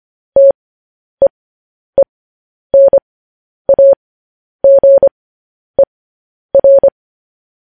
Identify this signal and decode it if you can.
Morse code, heard as a keyed tone, about 25 wpm, spaced as for 10 wpm, 550 Hz. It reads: TEENAGER